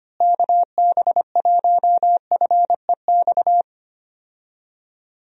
Morse code, heard as a keyed tone, about 25 wpm, 700 Hz. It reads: KB1FEX
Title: USA Call Signs - 6 Characters - Repeated 2x - 25wpm